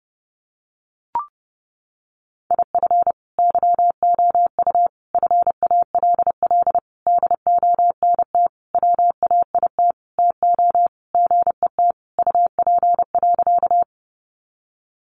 Morse code, heard as a keyed tone, about 30 wpm, 700 Hz. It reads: IF YOU FALL DONT WAIT TO GET UP.
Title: Sentences from Top 500 Words 30wpm